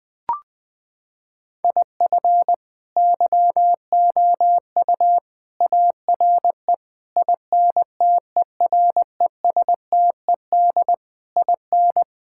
Morse code, heard as a keyed tone, about 21 wpm, 700 Hz. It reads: IF YOU ARE INTERESTED IN
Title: Sets of 8 Words - 20wpm